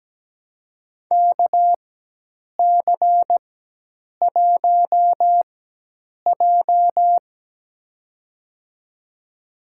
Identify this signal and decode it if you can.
Morse code, heard as a keyed tone, about 17 wpm, spaced as for 8 wpm, 700 Hz. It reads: KC1J
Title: USA Call Signs - 4 Characters - 4x Character Spacing - 17wpm